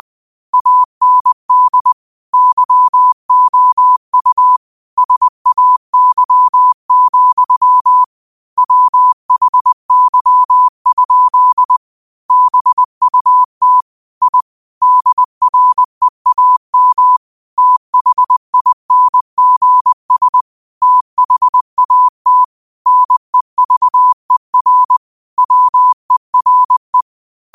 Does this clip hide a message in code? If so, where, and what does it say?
Yes: Morse code from 0.5 to 27.0 s, heard as a keyed tone, about 20 wpm, 1000 Hz. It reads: AND YOU SAY, WHY? BUT I DREAM THINGS THAT NEVER WERE